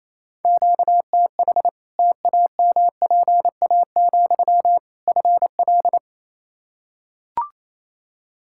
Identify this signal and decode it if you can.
Morse code, heard as a keyed tone, about 28 wpm, 700 Hz. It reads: QTH TAMPA, FL
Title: Practice QSO - Exchange 1 - Beginning - 28wpm